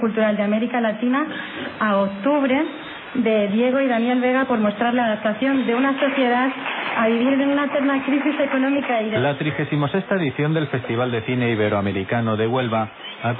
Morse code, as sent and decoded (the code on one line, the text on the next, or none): none